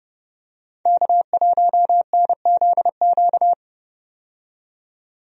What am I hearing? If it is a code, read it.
Morse code, heard as a keyed tone, about 30 wpm, 700 Hz. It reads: K1NZQ
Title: USA Call Signs - 5 Characters - 30wpm